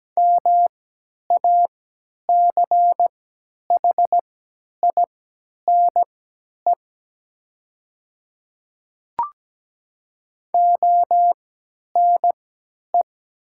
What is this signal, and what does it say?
Morse code, heard as a keyed tone, about 17 wpm, spaced as for 10 wpm, 700 Hz. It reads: MACHINE ONE
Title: 3x Character Spacing - Top 500 Words - 17wpm